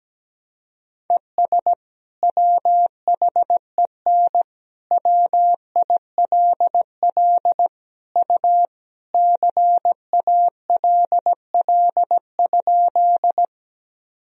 Morse code, heard as a keyed tone, about 17 wpm, 700 Hz. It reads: ES WHEN WILL U CALL?
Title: Sentences from Top 100 Words and Abbreviations 17wpm